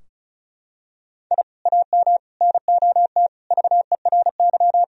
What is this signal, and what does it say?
Morse code, heard as a keyed tone, about 34 wpm, 700 Hz. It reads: I AM NOT VERY
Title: Sentences from Top 1000 Words - Review Part 4 - 35wpm